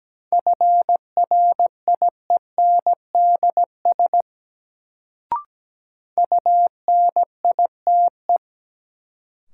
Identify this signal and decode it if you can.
Morse code, heard as a keyed tone, about 17 wpm, 700 Hz. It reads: FRIENDS UNITE
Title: Sentences from Top 1000 Words - Review Part 5 - New Word Drill - 17wpm